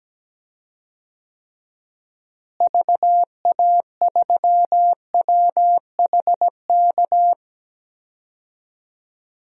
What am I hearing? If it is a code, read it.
Morse code, heard as a keyed tone, about 17 wpm, 700 Hz. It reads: VA3WHK
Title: DXCC-100 Call Signs - 6 Characters - Repeated 2x - 17wpm